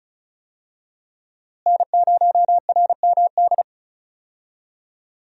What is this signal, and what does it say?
Morse code, heard as a keyed tone, about 36 wpm, 700 Hz. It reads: N0RMD
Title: USA Call Signs - 5 Characters - Repeated 3x - 35wpm